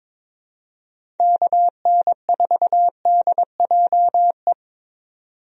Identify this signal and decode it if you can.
Morse code, heard as a keyed tone, about 22 wpm, 700 Hz. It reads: KN4DJE
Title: USA Call Signs - 6 Characters - 22wpm